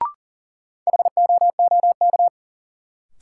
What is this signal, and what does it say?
Morse code, heard as a keyed tone, about 40 wpm, 700 Hz. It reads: HOOK
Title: Sentences from Top 1600 Words - New Word Drill - 40wpm